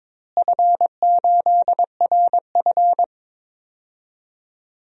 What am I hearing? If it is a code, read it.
Morse code, heard as a keyed tone, about 22 wpm, 700 Hz. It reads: F8RF
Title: DXCC-100 Call Signs - 4 Characters - Repeated 2x - 22wpm